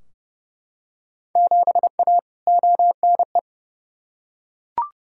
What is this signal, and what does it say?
Morse code, heard as a keyed tone, about 30 wpm, 700 Hz. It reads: 7A ONE
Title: Field Day Exchanges Encore - 30wpm